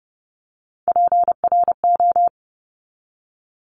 Morse code, heard as a keyed tone, about 30 wpm, 700 Hz. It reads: PRO
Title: Top Three Letter Combinations in Words 30wpm